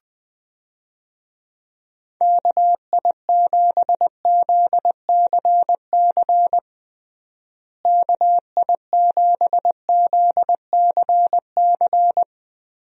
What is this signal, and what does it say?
Morse code, heard as a keyed tone, about 20 wpm, 700 Hz. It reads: KI7ZCC KI7ZCC
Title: USA Call Signs - 6 Characters - Repeated 2x - 20wpm